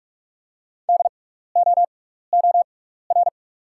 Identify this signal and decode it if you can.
Morse code, heard as a keyed tone, about 47 wpm, spaced as for 16 wpm, 690 Hz. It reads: DOOR
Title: 6x Character Spacing - Top 300 Words - 45wpm